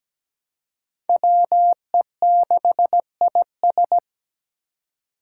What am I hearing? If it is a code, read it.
Morse code, heard as a keyed tone, about 17 wpm, 700 Hz. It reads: WE6IS
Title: Five Challenging Letter-Number Combinations 17wpm